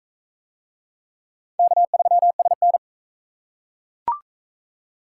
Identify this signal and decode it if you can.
Morse code, heard as a keyed tone, about 40 wpm, 700 Hz. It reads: K3SN